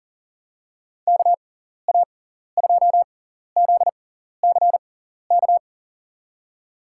Morse code, heard as a keyed tone, about 40 wpm, spaced as for 14 wpm, 700 Hz. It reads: KA2ZCK
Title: USA Call Signs - 6 Characters - 6x Character Spacing - 40wpm